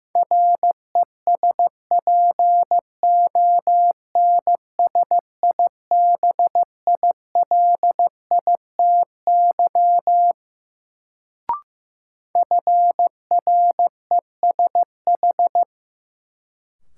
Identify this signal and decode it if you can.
Morse code, heard as a keyed tone, about 16 wpm, 700 Hz. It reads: RESPONSIBILITY FRESH
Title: Sentences from Top 1200 Words - New Word Drill Alt - 15wpm